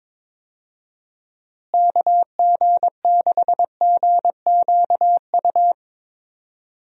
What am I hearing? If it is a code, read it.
Morse code, heard as a keyed tone, about 22 wpm, 700 Hz. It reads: KG6GQU
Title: USA Call Signs - 6 Characters - Repeated 3x - 22wpm